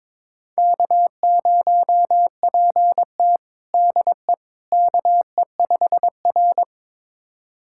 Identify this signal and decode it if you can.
Morse code, heard as a keyed tone, about 22 wpm, 700 Hz. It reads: K0PT DE KE5R
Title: USA Callsign DE Callsign - 4 Characters - 22wpm